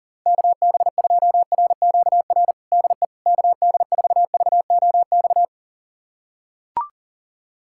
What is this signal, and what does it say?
Morse code, heard as a keyed tone, about 37 wpm, 700 Hz. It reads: KB2RQR DE KD4VOX
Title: USA Callsign DE Callsign - 6 Characters - 40wpm